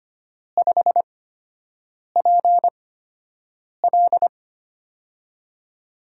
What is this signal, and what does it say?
Morse code, heard as a keyed tone, about 25 wpm, spaced as for 7 wpm, 700 Hz. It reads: 5PL
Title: USA Call Signs - 4 Characters - 8x Character Spacing - 25wpm